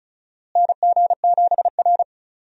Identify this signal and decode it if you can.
Morse code, heard as a keyed tone, about 36 wpm, 700 Hz. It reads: NG7R
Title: USA Call Signs - 4 Characters - Repeated 2x - 35wpm